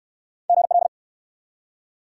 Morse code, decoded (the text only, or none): GL